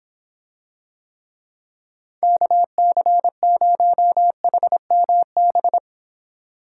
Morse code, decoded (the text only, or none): KC0HMB